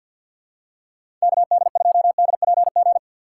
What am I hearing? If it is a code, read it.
Morse code, heard as a keyed tone, about 49 wpm, 700 Hz. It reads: KD2DPG
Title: USA Call Signs - 6 Characters - 3x Character Spacing - 50wpm